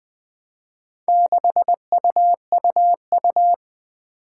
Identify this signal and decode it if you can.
Morse code, heard as a keyed tone, about 20 wpm, 700 Hz. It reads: 6UUU
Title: Four Challenging Letter-Number Combinations 20wpm